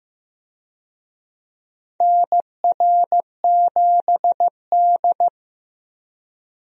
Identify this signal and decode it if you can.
Morse code, heard as a keyed tone, about 15 wpm, 700 Hz. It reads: NR7D
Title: USA Call Signs - 4 Characters - Repeated 3x - 15wpm